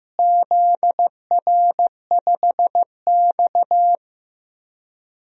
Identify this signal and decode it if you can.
Morse code, heard as a keyed tone, about 15 wpm, 700 Hz. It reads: ZR5X
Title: DXCC-100 Call Signs - 4 Characters - Repeated 3x - 15wpm